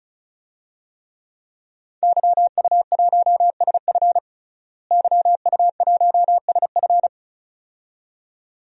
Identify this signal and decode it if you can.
Morse code, heard as a keyed tone, about 36 wpm, 700 Hz. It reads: YU1SF YU1SF
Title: DXCC-100 Call Signs - 5 Characters - Repeated 2x - 35wpm